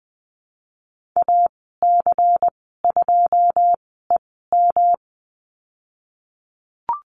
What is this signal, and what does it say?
Morse code, heard as a keyed tone, about 20 wpm, spaced as for 15 wpm, 700 Hz. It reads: AC2EM
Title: 128 Lesson 21g - Number 6 - Call Signs 20wpm